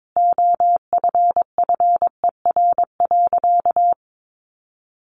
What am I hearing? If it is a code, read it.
Morse code, heard as a keyed tone, about 22 wpm, 700 Hz. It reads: OFFER.
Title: Sentences from Top 1100 Words 22wpm